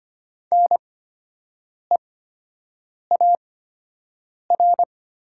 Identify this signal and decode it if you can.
Morse code, heard as a keyed tone, about 25 wpm, spaced as for 7 wpm, 700 Hz. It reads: NEAR
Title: 8x Character Spacing - Top 200 Words - 25wpm